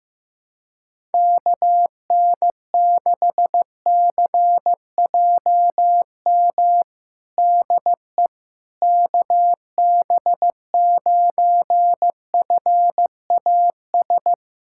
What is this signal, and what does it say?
Morse code, heard as a keyed tone, about 15 wpm, 700 Hz. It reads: KN6CJM DE KB9FAS
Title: USA Callsign DE Callsign 15wpm